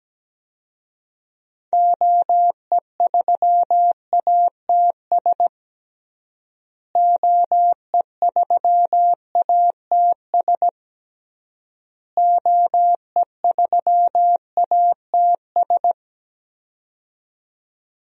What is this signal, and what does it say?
Morse code, heard as a keyed tone, about 17 wpm, 700 Hz. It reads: OE3ATS OE3ATS OE3ATS
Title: DXCC-100 Call Signs - 6 Characters - Repeated 3x - 17wpm